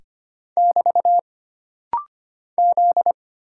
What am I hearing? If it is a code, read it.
Morse code, heard as a keyed tone, about 25 wpm, 700 Hz. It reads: <BT> Z
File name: Single Character (Letter:Number:Punctuation:Prosign) - Rapid-Fire - 25wpm